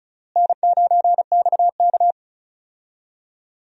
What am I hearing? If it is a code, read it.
Morse code, heard as a keyed tone, about 34 wpm, 700 Hz. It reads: N9XK
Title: USA Call Signs - 4 Characters - Repeated 3x - 35wpm